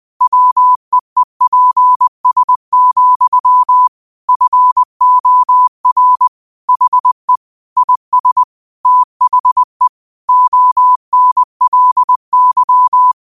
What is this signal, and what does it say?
Morse code, heard as a keyed tone, about 20 wpm, 1000 Hz. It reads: WEEPS, FOR HE IS THE ONLY
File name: qotdfiles / 20wpm